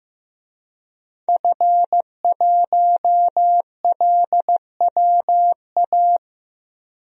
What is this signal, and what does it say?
Morse code, heard as a keyed tone, about 15 wpm, 700 Hz. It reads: F1LWA
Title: DXCC-100 Call Signs - 5 Characters - 15wpm